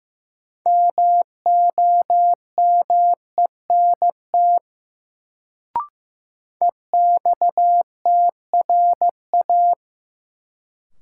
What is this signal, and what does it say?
Morse code, heard as a keyed tone, about 15 wpm, 700 Hz. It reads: MOMENT EXTRA
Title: Sentences from Top 1000 Words - New Word Drill - 15wpm